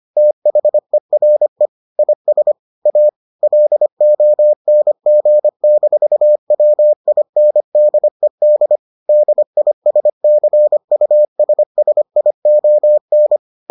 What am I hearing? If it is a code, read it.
Morse code, heard as a keyed tone, about 25 wpm, 600 Hz. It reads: THERE IS A LONG-WINDED DISCUSSION